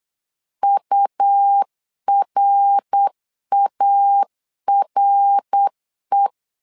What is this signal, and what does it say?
Morse code, heard as a keyed tone, about 8 wpm, 800 Hz. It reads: URARE
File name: LEZIONI / 3 MWRO / 3 MWR 8wpm 2444